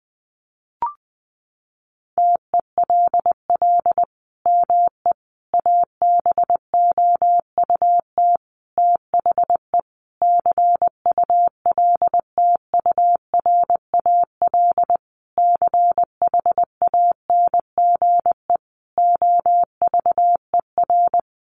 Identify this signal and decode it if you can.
Morse code, heard as a keyed tone, about 20 wpm, 700 Hz. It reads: TELL ME ABOUT THE CULTURAL CHANGE OVER
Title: Sentences from Top 1500 Words 20wpm